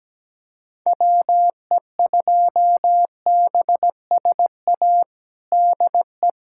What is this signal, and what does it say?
Morse code, heard as a keyed tone, about 17 wpm, 700 Hz. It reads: WE2BSA DE
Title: USA Callsign DE Callsign - 6 Characters - 17wpm